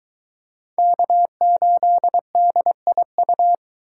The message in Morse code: K8DIU

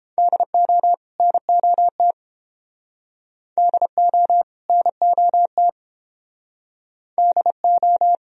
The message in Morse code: DO NOT DO NOT DO

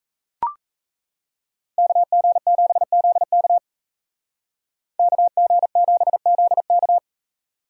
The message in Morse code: KG7ZK KG7ZK

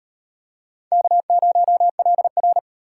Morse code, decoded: K0LR

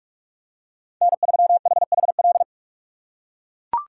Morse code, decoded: N3HHL

